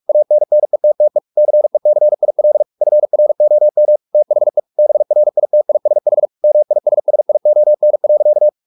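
Morse code, heard as a keyed tone, about 44 wpm, 600 Hz. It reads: ANNETTE CECIL FROM THE BRITISH MISSION.
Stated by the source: Morse practice at 45 wpm, the novel Never by Ken Follett in English, from CW med Gnister.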